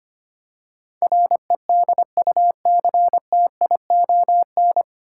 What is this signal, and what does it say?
Morse code, heard as a keyed tone, about 25 wpm, 700 Hz. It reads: REDUCTION
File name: practice / 25wpm / 9 Letter Words - First 3 Letters Given - 25wpm